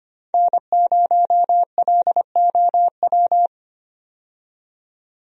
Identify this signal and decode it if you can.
Morse code, heard as a keyed tone, about 25 wpm, 700 Hz. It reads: N0LOW
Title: USA Call Signs - 5 Characters - 25wpm